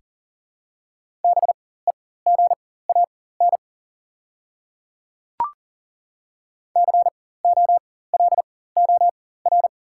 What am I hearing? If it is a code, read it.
Morse code, heard as a keyed tone, about 40 wpm, spaced as for 19 wpm, 690 Hz. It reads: BEGAN COLOR
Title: 4x Character Spacing - Top 300 Words - 40wpm